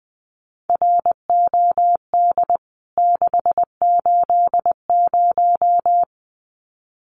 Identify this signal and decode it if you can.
Morse code, heard as a keyed tone, about 20 wpm, 700 Hz. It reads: ROD 680
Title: CWT Exchange 20wpm